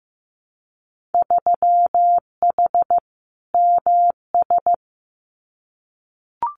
Morse code, heard as a keyed tone, about 15 wpm, 700 Hz. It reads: 3H MS
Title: Winter Field Day Exchanges Encore 15wpm